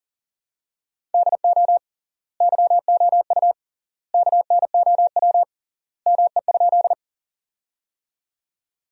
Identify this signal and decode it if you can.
Morse code, heard as a keyed tone, about 40 wpm, 700 Hz. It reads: DO YOU KNOW ME?
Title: 3x Word-Spacing - Sentences from Top 200 Words - 40wpm